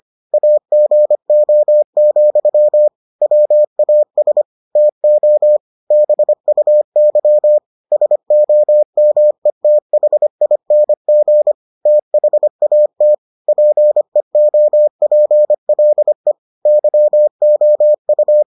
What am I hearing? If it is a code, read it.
Morse code, heard as a keyed tone, about 25 wpm, 600 Hz. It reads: AGO, WAS TO BUY SOMETHING THAT PEOPLE YOU